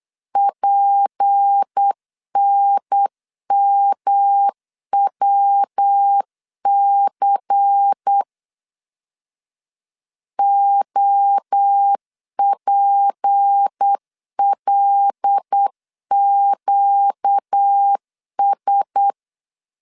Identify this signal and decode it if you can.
Morse code, heard as a keyed tone, about 8 wpm, 800 Hz. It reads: PNMWC OPLQS
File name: LEZIONI / 6 PKX0 / 6 P 8wpm 3025